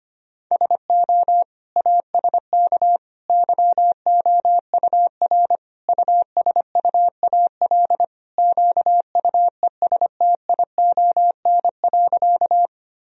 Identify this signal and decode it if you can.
Morse code, heard as a keyed tone, about 25 wpm, 700 Hz. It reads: SO ASK YOUR USUAL QUESTION.
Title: Sentences from Top 1000 Words - Review Part 2 - 25wpm